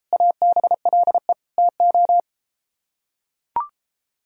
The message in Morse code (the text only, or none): ABLE TO